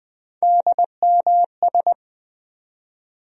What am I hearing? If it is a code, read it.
Morse code, heard as a keyed tone, about 20 wpm, 700 Hz. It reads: DMS
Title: USA Call Sign Postfixes - 3 Letters 20wpm